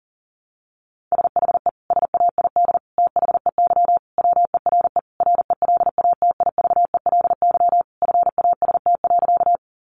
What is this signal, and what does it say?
Morse code, heard as a keyed tone, about 40 wpm, 700 Hz. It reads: SHE SAID THEY WERE RELATIVELY FAST.
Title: Sentences from Top 1100 Words 40wpm